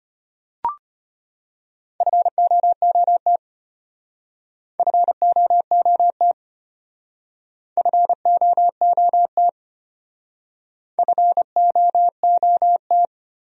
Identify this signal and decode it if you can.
Morse code, heard as a keyed tone, about 33 wpm, 700 Hz. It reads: FOOT FOOT FOOT FOOT